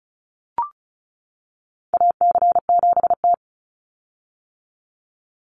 Morse code, heard as a keyed tone, about 34 wpm, 700 Hz. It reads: AC7T
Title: Call Signs - 35wpm